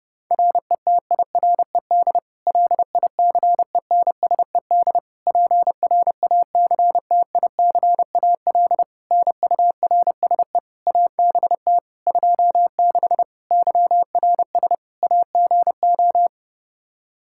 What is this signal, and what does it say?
Morse code, heard as a keyed tone, about 30 wpm, 700 Hz. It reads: RETIRED LICENSED PRACTICAL NURSE ABT 26 YRS AGO
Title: Practice QSO - Exchange 3 - More Info - 30wpm